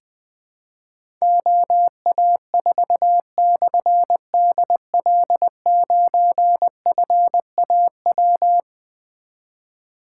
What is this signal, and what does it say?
Morse code, heard as a keyed tone, about 20 wpm, 700 Hz. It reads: OA4/DL9FAW
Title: Difficult Call Signs 20wpm